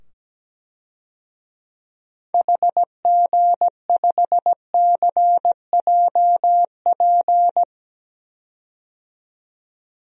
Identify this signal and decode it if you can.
Morse code, heard as a keyed tone, about 17 wpm, 700 Hz. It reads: HG5CJP
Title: DXCC-100 Call Signs - 6 Characters - 17wpm